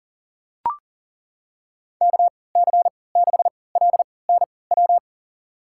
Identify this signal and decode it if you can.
Morse code, heard as a keyed tone, about 40 wpm, spaced as for 23 wpm, 700 Hz. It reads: KC6LNW
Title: USA Call Signs - 6 Characters - 3x Character Spacing - 40wpm